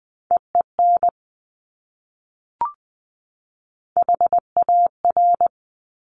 Morse code, heard as a keyed tone, about 20 wpm, 700 Hz. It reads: EEN HAR
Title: Top Three Letter Combinations in Words 20wpm